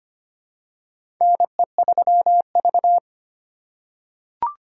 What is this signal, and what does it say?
Morse code, heard as a keyed tone, about 25 wpm, 700 Hz. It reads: NE3V
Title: USA Call Signs - 4 Characters - 2x Character Spacing - 25wpm